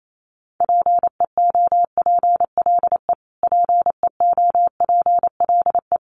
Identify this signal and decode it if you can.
Morse code, heard as a keyed tone, about 28 wpm, 700 Hz. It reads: PEOPLE PEOPLE
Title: Top 200 Words Repeated 28wpm